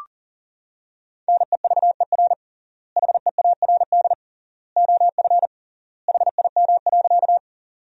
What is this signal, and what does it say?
Morse code, heard as a keyed tone, about 40 wpm, 700 Hz. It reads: NEVER HEARD OF HIM.
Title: 3x Word-Spacing - Sentences from Top 400 Words - 40wpm